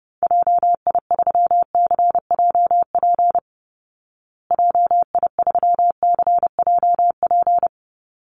Morse code, heard as a keyed tone, about 30 wpm, 700 Hz. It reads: JI3CJP JI3CJP
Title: Call Signs Repeated 3x - 30wpm